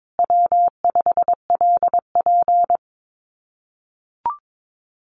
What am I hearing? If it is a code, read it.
Morse code, heard as a keyed tone, about 22 wpm, 700 Hz. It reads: W5LP
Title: Common QSO Phrases Encore 22wpm